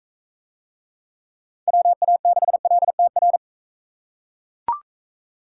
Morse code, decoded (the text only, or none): WA6LTR